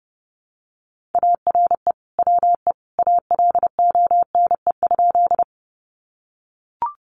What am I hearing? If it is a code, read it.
Morse code, heard as a keyed tone, about 30 wpm, 700 Hz. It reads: ARE WE ALONE?
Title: Sentences from Top 1000 Words 30wpm